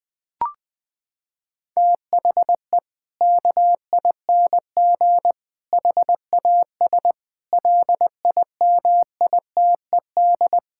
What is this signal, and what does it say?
Morse code, heard as a keyed tone, about 20 wpm, 700 Hz. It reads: THE KING HAS LIMITED
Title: Sentences from Top 1000 Words - Review Part 6 - 20wpm